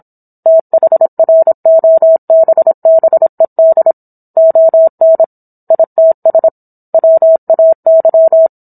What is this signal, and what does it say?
Morse code, heard as a keyed tone, about 27 wpm, 650 Hz. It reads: THROBBED ON ITS WAY
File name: Book One / 17-Chapter 17 - 26 WPM